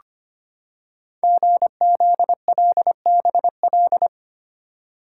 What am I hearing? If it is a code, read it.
Morse code, heard as a keyed tone, about 25 wpm, 700 Hz. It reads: GZLBL
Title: Code Groups - 5 Letters - 25wpm